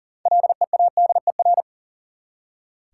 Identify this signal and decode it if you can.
Morse code, heard as a keyed tone, about 40 wpm, 700 Hz. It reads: LEADER
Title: Six Letter Words - 40wpm